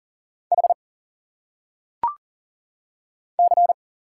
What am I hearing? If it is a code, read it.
Morse code, heard as a keyed tone, about 40 wpm, 700 Hz. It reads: H C
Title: Single Letters - Phonetic Pronunciation 40wpm